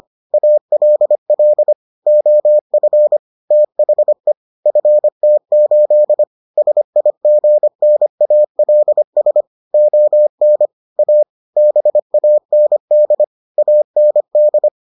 Morse code, heard as a keyed tone, about 25 wpm, 600 Hz. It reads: ALL OF THE FT8 SIGNALS ON A BAND AND